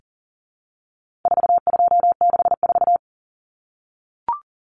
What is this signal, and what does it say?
Morse code, heard as a keyed tone, about 40 wpm, 700 Hz. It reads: 4264
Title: Common Cut Numbers - 4-Digit 40wpm